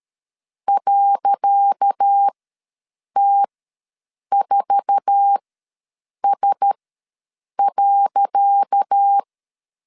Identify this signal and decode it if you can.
Morse code, heard as a keyed tone, about 13 wpm, spaced as for 7 wpm, 800 Hz. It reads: .T4S.
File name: LEZIONI / 10 SIGNS / 10C DOT ws 12wpm 3548